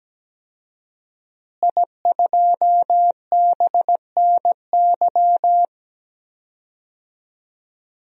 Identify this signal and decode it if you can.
Morse code, heard as a keyed tone, about 17 wpm, 700 Hz. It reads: I2BNY
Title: DXCC-100 Call Signs - 5 Characters - Repeated 3x - 17wpm